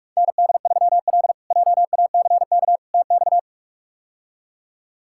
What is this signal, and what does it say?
Morse code, heard as a keyed tone, about 46 wpm, 700 Hz. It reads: ND3L JACK TX
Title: K1USN SST QSO Practice - 45wpm